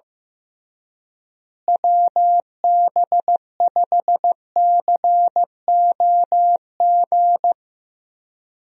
Morse code, heard as a keyed tone, about 15 wpm, 700 Hz. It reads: WB5COG
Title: USA Call Signs - 6 Characters - Repeated 3x - 15wpm